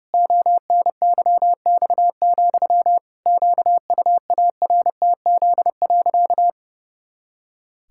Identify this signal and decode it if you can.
Morse code, heard as a keyed tone, about 30 wpm, 700 Hz. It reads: ONYX, QUARTZ.
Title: Pangram Challenge v2 - 30wpm